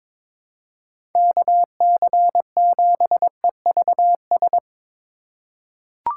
Morse code, heard as a keyed tone, about 22 wpm, 700 Hz. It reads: KC7EVS
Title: USA Call Signs - 6 Characters - 4x Character Spacing - 22wpm